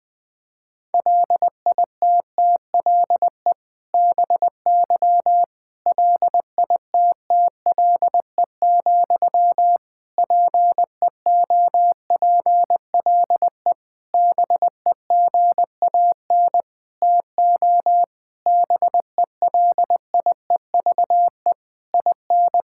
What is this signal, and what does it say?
Morse code, heard as a keyed tone, about 21 wpm, 700 Hz. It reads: LITTLE BY LITTLE, PEOPLE BEGAN TO BELIEVE IN
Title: Sentences from Top 1600 Words 20wpm